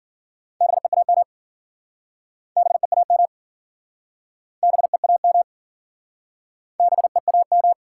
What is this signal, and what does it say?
Morse code, heard as a keyed tone, about 52 wpm, 700 Hz. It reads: BEAM BEAM BEAM BEAM